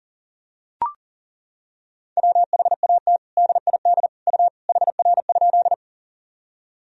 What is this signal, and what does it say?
Morse code, heard as a keyed tone, about 40 wpm, 700 Hz. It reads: WHAT DID U HR?
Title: Sentences from Top 300 Words and Abbreviations 40wpm